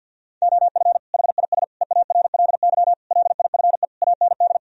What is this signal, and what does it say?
Morse code, heard as a keyed tone, about 50 wpm, 700 Hz. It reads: OF HIS EARLY LIFE AND
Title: Sentences from Top 1000 Words - Review Part 4 - 50wpm